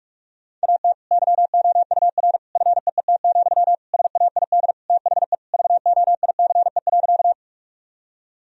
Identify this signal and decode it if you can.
Morse code, heard as a keyed tone, about 46 wpm, 700 Hz. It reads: AT YOUR FEET, SAID THE VOICE.
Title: Sentences from Top 1000 Words - Review Part 1 - 45wpm